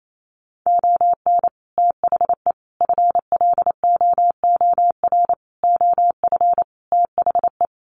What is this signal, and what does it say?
Morse code, heard as a keyed tone, about 28 wpm, 700 Hz. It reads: ON THE FLOOR OF THE
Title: Sets of 5 Words 28wpm